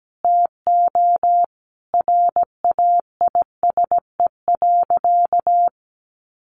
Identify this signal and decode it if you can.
Morse code, heard as a keyed tone, about 17 wpm, 700 Hz. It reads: TO RAISE.
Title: Sentences from Top 1400 Words 17wpm